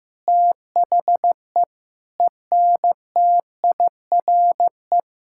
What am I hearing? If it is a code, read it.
Morse code, heard as a keyed tone, about 15 wpm, 700 Hz. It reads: THE ENTIRE
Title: Sentences from Top 1000 Words - Review Part 5 - 15wpm